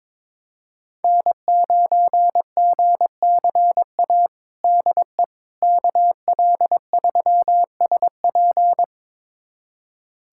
Morse code, heard as a keyed tone, about 22 wpm, 700 Hz. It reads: N9GCA DE KL3SP
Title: USA Callsign DE Callsign - 5 Characters - 22wpm